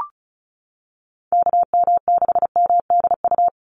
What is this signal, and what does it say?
Morse code, heard as a keyed tone, about 34 wpm, 700 Hz. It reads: KM6MDU
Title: USA Call Signs 35wpm